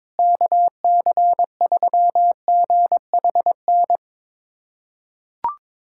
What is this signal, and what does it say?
Morse code, heard as a keyed tone, about 22 wpm, 700 Hz. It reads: KC3GHN
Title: USA Callsign DE Callsign - 6 Characters - 22wpm